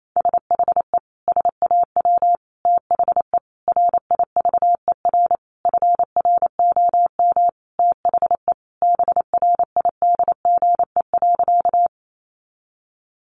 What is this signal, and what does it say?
Morse code, heard as a keyed tone, about 28 wpm, 700 Hz. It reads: SHE SAW THE RIVER FROM THE BRIDGE.
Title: Sentences from Top 1200 Words 28wpm